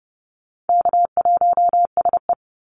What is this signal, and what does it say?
Morse code, heard as a keyed tone, about 30 wpm, 700 Hz. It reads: K1SE
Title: Easy Call Signs - 30wpm